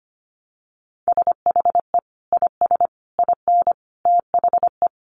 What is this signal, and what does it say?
Morse code, heard as a keyed tone, about 25 wpm, 700 Hz. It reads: SHE IS IN THE
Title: Sentences from Top 1200 Words 25wpm